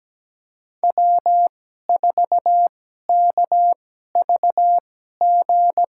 Morse code, heard as a keyed tone, about 17 wpm, spaced as for 12 wpm, 700 Hz. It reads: W4KVG
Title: USA Call Signs - 2x Character Spacing - 17wpm